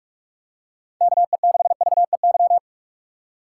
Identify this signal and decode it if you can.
Morse code, heard as a keyed tone, about 43 wpm, 700 Hz. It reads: KE6VEY
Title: USA Call Signs - 6 Characters - 2x Character Spacing - 45wpm